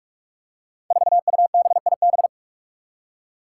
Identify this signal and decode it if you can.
Morse code, heard as a keyed tone, about 45 wpm, 700 Hz. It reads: 4UBIB
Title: Five Challenging Letter-Number Combinations 45wpm